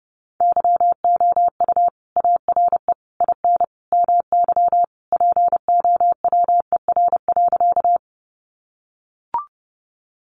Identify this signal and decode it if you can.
Morse code, heard as a keyed tone, about 30 wpm, 700 Hz. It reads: YOU ARE IN MY POWER.
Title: Sentences from Top 500 Words 30wpm